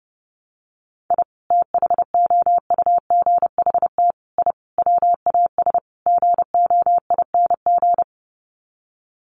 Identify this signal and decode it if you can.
Morse code, heard as a keyed tone, about 30 wpm, 700 Hz. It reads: I THOUGHT I WAS GOING
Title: Sets of 5 Words Encore2 30wpm